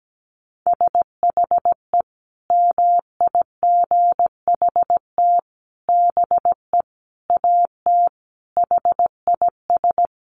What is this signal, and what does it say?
Morse code, heard as a keyed tone, about 17 wpm, 700 Hz. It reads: SHE MIGHT BE AT HIS
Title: Sentences from Top 300 Words 17wpm